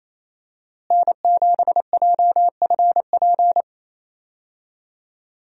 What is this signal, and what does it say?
Morse code, heard as a keyed tone, about 28 wpm, 700 Hz. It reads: N7JFP
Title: USA Call Signs - 5 Characters - Repeated 3x - 28wpm